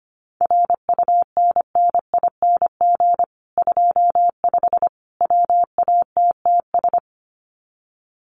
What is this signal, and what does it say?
Morse code, heard as a keyed tone, about 25 wpm, 700 Hz. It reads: RUNNING 25 WATTS
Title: Common QSO Phrases 25wpm